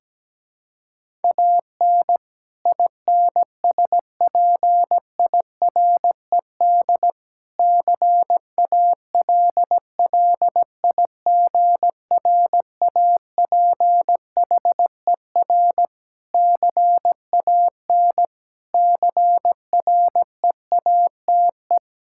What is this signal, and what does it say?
Morse code, heard as a keyed tone, about 17 wpm, 700 Hz. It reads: AN INSPIRED CALLIGRAPHER CAN CREATE
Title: Pangram Challenge v2 - 17wpm